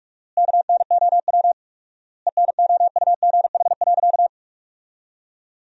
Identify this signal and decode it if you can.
Morse code, heard as a keyed tone, about 48 wpm, 700 Hz. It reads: KNOW ENOUGH.
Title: 4x Word-Spacing - Sentences from Top 400 Words - 45wpm